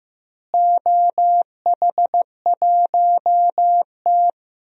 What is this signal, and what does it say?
Morse code, heard as a keyed tone, about 15 wpm, 700 Hz. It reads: OH1T
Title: DXCC-100 Call Signs - 4 Characters - Repeated 3x - 15wpm